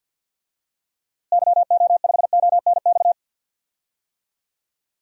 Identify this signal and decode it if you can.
Morse code, heard as a keyed tone, about 50 wpm, 700 Hz. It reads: YO5ONX
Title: DXCC-100 Call Signs - 6 Characters - Repeated 2x - 50wpm